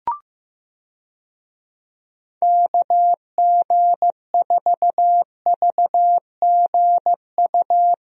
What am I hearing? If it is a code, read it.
Morse code, heard as a keyed tone, about 15 wpm, 700 Hz. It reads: KG4VGU
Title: USA Call Signs - 6 Characters - Repeated 2x - 15wpm